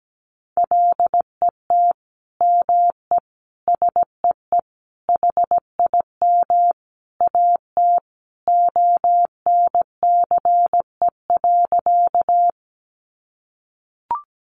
Morse code, heard as a keyed tone, about 17 wpm, 700 Hz. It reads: LET ME SEE HIM AT ONCE.
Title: Sentences from Top 300 Words 17wpm